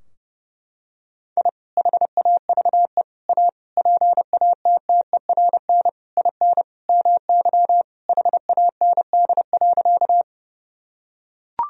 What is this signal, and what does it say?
Morse code, heard as a keyed tone, about 31 wpm, 700 Hz. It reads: I HAVE A PATTERN IN MY HAND.
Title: Sentences from Top 1000 Words - Review Part 2 - 30wpm